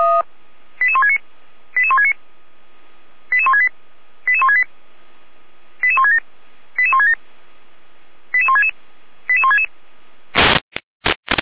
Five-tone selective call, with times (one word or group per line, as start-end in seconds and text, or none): 0.8-1.2 88169
1.8-2.1 88169
3.3-3.7 88168
4.3-4.6 88168
5.8-6.2 88167
6.8-7.1 88167
8.3-8.7 88166
9.3-9.7 88166